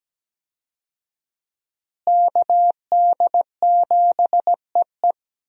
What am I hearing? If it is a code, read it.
Morse code, heard as a keyed tone, about 17 wpm, 700 Hz. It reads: KD7EE
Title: USA Call Signs - 5 Characters - Repeated 2x - 17wpm